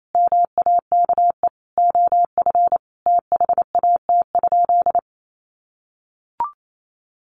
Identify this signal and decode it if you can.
Morse code, heard as a keyed tone, about 28 wpm, 700 Hz. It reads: MAKE OF THAT?
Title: Sentences from Top 100 Words 28wpm